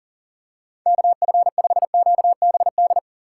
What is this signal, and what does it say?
Morse code, heard as a keyed tone, about 40 wpm, 700 Hz. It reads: KF5QBD
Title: USA Call Signs - 6 Characters - 6x Character Spacing - 40wpm